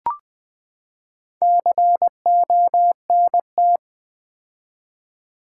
Morse code, heard as a keyed tone, about 20 wpm, 700 Hz. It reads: CONT
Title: Build A Word - 4 Letters At A Time - 10-Letter Words - 20wpm